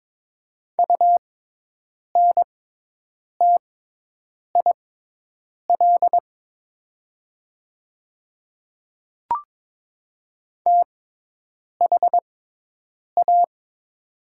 Morse code, heard as a keyed tone, about 22 wpm, spaced as for 8 wpm, 700 Hz. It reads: UNTIL THA
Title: 6x Character Spacing - Top 300 Words - 22wpm